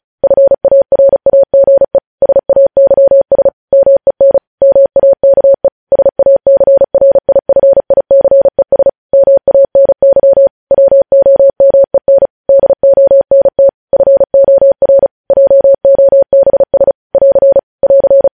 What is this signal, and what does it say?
Morse code, heard as a keyed tone, about 36 wpm, 550 Hz. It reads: FARAGE SAYS MEN MAKE SACRIFICES MANY WOMEN DONT FOR JOBS <AR> <AR>
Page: News headlines in Morse code at 35 wpm